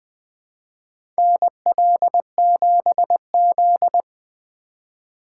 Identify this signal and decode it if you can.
Morse code, heard as a keyed tone, about 20 wpm, 700 Hz. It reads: NL7Z
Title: USA Call Signs - 4 Characters - 4x Character Spacing - 20wpm